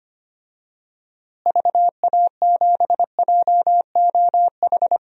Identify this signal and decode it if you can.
Morse code, heard as a keyed tone, about 25 wpm, 700 Hz. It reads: VA7JOH
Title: DXCC-100 Call Signs - 6 Characters - Repeated 2x - 25wpm